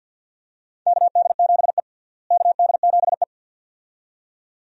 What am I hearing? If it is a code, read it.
Morse code, heard as a keyed tone, about 50 wpm, 700 Hz. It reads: KD7E KD7E
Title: USA Call Signs - 4 Characters - Repeated 2x - 50wpm